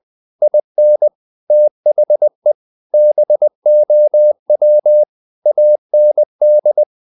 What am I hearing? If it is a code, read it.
Morse code, heard as a keyed tone, about 20 wpm, 600 Hz. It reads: IN THE BOW AND
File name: the old man and the sea part 2 20wpm